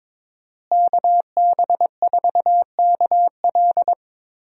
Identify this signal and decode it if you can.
Morse code, heard as a keyed tone, about 22 wpm, 700 Hz. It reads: KB4KL
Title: USA Call Signs - 5 Characters - Repeated 3x - 22wpm